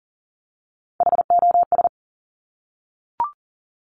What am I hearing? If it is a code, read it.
Morse code, heard as a keyed tone, about 41 wpm, 700 Hz. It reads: HOS
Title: Top Three Letter Combinations in Words 40wpm